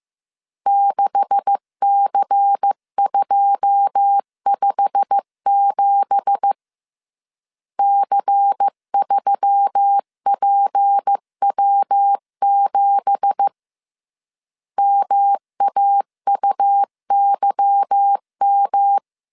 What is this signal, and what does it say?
Morse code, heard as a keyed tone, about 15 wpm, 800 Hz. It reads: 6C257 C3PW7 MAUYM
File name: CW / DET / 1 9 1500C S B 14wpm 3302